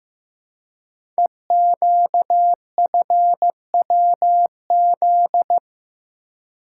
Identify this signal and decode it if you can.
Morse code, heard as a keyed tone, about 15 wpm, 700 Hz. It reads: EQFWZ
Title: Code Groups - 5 Letter:Numbers - 15wpm